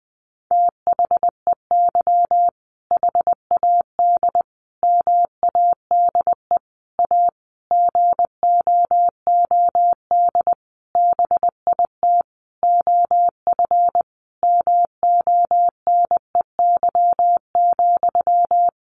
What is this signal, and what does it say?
Morse code, heard as a keyed tone, about 20 wpm, 700 Hz. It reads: THEY HAD MADE A GOOD BIT OF MONEY,
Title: Sentences from Top 900 Words 20wpm